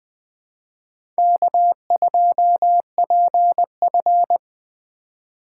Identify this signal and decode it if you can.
Morse code, heard as a keyed tone, about 20 wpm, 700 Hz. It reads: K2PF
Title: USA Call Signs - 4x Character Spacing - 20wpm